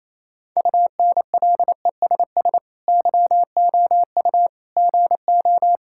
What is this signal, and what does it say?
Morse code, heard as a keyed tone, about 28 wpm, 700 Hz. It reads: UNLESS YOU GO
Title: Sentences from Top 1000 Words - Review Part 6 - 28wpm